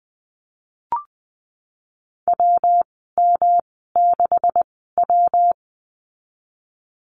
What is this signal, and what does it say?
Morse code, heard as a keyed tone, about 20 wpm, spaced as for 15 wpm, 700 Hz. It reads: WM6W
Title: 128 Lesson 21g - Number 6 - Call Signs 20wpm